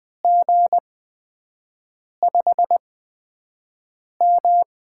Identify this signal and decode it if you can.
Morse code, heard as a keyed tone, about 20 wpm, spaced as for 5 wpm, 700 Hz. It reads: G5M
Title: USA Call Signs - 4 Characters - 8x Character Spacing - 20wpm